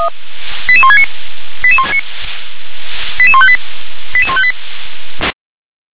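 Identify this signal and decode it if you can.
Five-tone selective call (ZVEI), heard as quick runs of five tones, tones of about 70 ms. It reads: 88169 88158 88158
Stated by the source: ZVEI five-tone selective call received by SDR.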